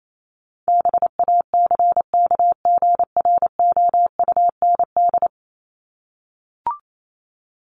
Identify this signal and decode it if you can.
Morse code, heard as a keyed tone, about 28 wpm, 700 Hz. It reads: BACKGROUND
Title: Top 1001-1250 Words 28wpm